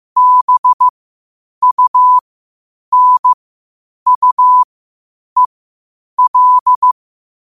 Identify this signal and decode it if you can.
Morse code, heard as a keyed tone, about 15 wpm, spaced as for 8 wpm, 1000 Hz. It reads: BUNUEL
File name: qotdfiles / 8wpm